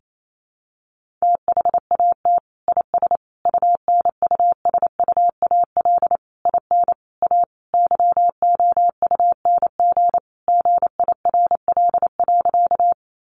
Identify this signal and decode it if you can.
Morse code, heard as a keyed tone, about 28 wpm, 700 Hz. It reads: THAT IS UNUSUAL IN A YOUNG GIRL.
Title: Sentences from Top 1400 Words 28wpm